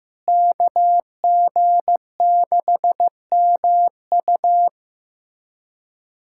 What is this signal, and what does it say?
Morse code, heard as a keyed tone, about 16 wpm, 700 Hz. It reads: KG6MU
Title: USA Callsign DE Callsign - 5 Characters - 15wpm